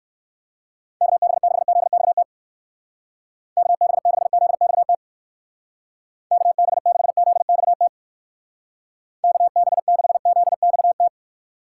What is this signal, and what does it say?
Morse code, heard as a keyed tone, about 53 wpm, 700 Hz. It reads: KB6ZXT KB6ZXT KB6ZXT KB6ZXT
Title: Speed Racing - USA 2x3 Call Signs 45wpm